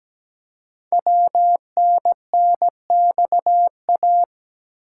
Morse code, heard as a keyed tone, about 17 wpm, 700 Hz. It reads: WNNXA
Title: Code Groups - 5 Letters - 17wpm